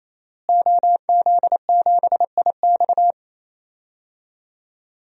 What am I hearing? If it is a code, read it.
Morse code, heard as a keyed tone, about 27 wpm, 700 Hz. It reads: OZ7IX
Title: DXCC-100 Call Signs - 5 Characters - Repeated 3x - 28wpm